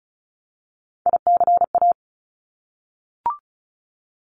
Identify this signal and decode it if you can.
Morse code, heard as a keyed tone, about 36 wpm, 700 Hz. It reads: ICA
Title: Top Three Letter Combinations in Words 35wpm